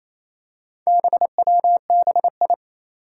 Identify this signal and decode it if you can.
Morse code, heard as a keyed tone, about 28 wpm, 700 Hz. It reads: BWBI
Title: Four Challenging Letter-Number Combinations 28wpm